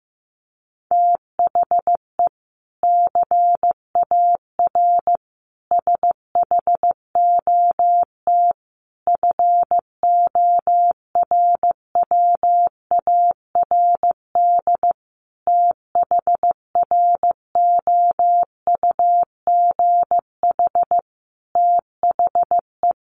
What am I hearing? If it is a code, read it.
Morse code, heard as a keyed tone, about 15 wpm, 700 Hz. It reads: THE CAR SHOT FORWARD THROUGH THE
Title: Sentences from Top 1300 Words 15wpm